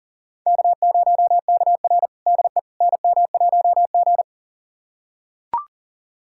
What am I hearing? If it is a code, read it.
Morse code, heard as a keyed tone, about 40 wpm, 700 Hz. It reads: K0KR DE NM1G
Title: USA Callsign DE Callsign - 4 Characters - 40wpm